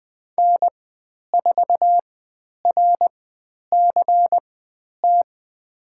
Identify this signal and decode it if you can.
Morse code, heard as a keyed tone, about 20 wpm, spaced as for 10 wpm, 700 Hz. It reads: N4RCT
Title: 095 Lesson 16g - Letter c - Call Signs 20wpm